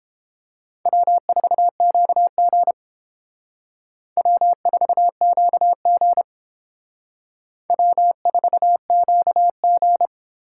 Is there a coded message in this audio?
Yes: Morse code from 0.9 to 10.1 s, heard as a keyed tone, about 30 wpm, 700 Hz. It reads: W4QG W4QG W4QG